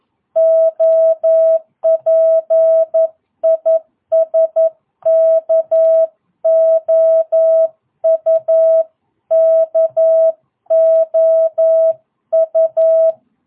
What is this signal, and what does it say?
Morse code, heard as a keyed tone, about 11 wpm, 660 Hz. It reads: OP IS KOU KOU
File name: jh8cbh / 2022cw